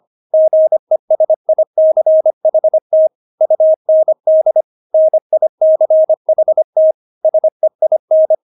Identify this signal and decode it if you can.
Morse code, heard as a keyed tone, about 25 wpm, 630 Hz. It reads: GESICHT UND NICHT SEIN
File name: Witz6 25wpm